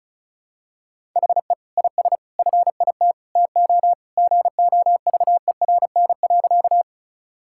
Morse code, heard as a keyed tone, about 36 wpm, 700 Hz. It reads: HE IS FIT TO GOVERN.
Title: Sentences from Top 1000 Words - Review Part 5 - 35wpm